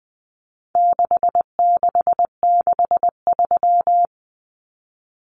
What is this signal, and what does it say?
Morse code, heard as a keyed tone, about 20 wpm, 700 Hz. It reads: 6663
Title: Numbers - 4-Digit 20wpm